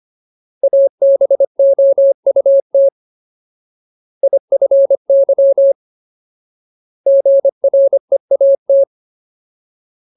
Morse code, heard as a keyed tone, about 25 wpm, 550 Hz. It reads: ABOUT IFY GREAT
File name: Ultimate Word List / 25WPM Ultimate